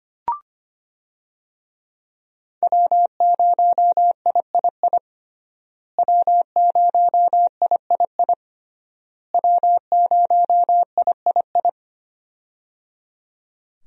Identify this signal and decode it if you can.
Morse code, heard as a keyed tone, about 25 wpm, 700 Hz. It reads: W0III W0III W0III
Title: USA Call Signs - 5 Characters - Repeated 3x - 25wpm